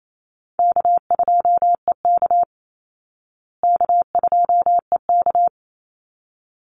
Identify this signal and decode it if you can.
Morse code, heard as a keyed tone, about 28 wpm, 700 Hz. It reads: K2EK K2EK
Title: Easy Call Signs Repeated 2x - 28wpm